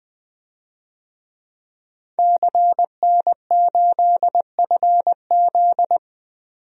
Morse code, heard as a keyed tone, about 20 wpm, 700 Hz. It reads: CN8FZ
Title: DXCC-100 Call Signs - 5 Characters - 20wpm